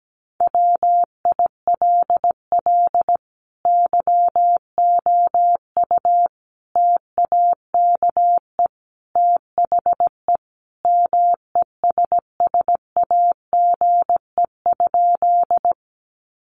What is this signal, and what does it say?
Morse code, heard as a keyed tone, about 17 wpm, 700 Hz. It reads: WILL YOU TAKE THE MESSAGE?
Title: Sentences from Top 1100 Words 17wpm